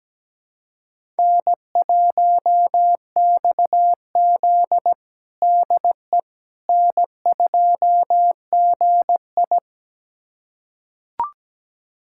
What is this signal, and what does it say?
Morse code, heard as a keyed tone, about 17 wpm, 700 Hz. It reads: N1XZ DE N2GI
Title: USA Callsign DE Callsign - 4 Characters - 17wpm